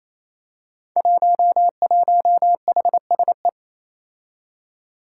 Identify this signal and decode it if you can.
Morse code, heard as a keyed tone, about 29 wpm, 700 Hz. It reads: 11HSE
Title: Five Challenging Letter-Number Combinations 28wpm